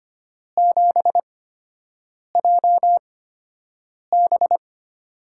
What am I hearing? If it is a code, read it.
Morse code, heard as a keyed tone, about 25 wpm, spaced as for 7 wpm, 700 Hz. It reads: 7JB
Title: USA Call Signs - 4 Characters - 8x Character Spacing - 25wpm